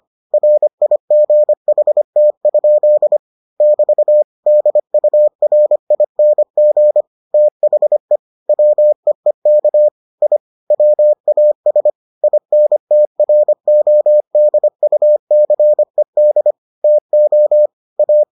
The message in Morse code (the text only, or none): RIGHT? <BT> DURING THE WEEK I WAS INTRODUCED TO A